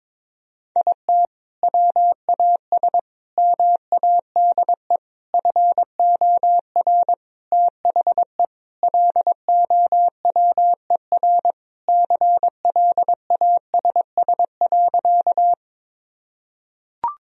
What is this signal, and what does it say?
Morse code, heard as a keyed tone, about 22 wpm, 700 Hz. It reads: IT WAS MADE FOR THE LOWER CLASS.
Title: Sentences from Top 1000 Words - Review Part 4 - 22wpm